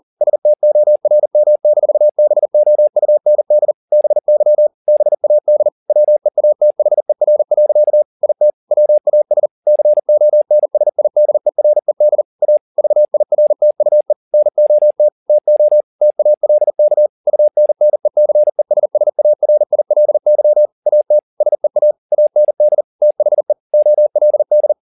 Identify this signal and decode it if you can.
Morse code, heard as a keyed tone, about 40 wpm, 600 Hz. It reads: STORM-BOUND BY BAD WEATHER. IT WAS CONSIDERED A VIRTUE NOT TO TALK UNNECESSARILY AT SEA AND THE OLD